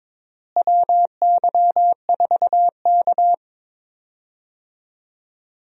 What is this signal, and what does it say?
Morse code, heard as a keyed tone, about 22 wpm, 700 Hz. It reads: WY4K
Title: USA Call Signs - 4 Characters - Repeated 2x - 22wpm